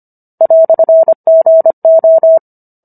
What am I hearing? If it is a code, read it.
Morse code, heard as a keyed tone, about 25 wpm, 650 Hz. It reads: "GO